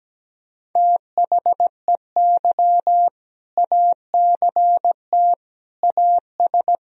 Morse code, heard as a keyed tone, about 17 wpm, 700 Hz. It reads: THEY ACT AS
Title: Sentences from Top 1000 Words - Review Part 2 - 17wpm